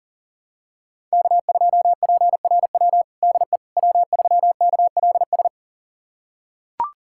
Morse code, heard as a keyed tone, about 37 wpm, 700 Hz. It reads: K2PRW DE W3KLS
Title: USA Callsign DE Callsign - 5 Characters - 40wpm